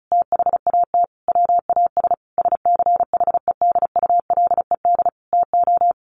Morse code, heard as a keyed tone, about 34 wpm, 700 Hz. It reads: THAT WAS SCHEDULED TO
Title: A Sound Of Thunder 35wpm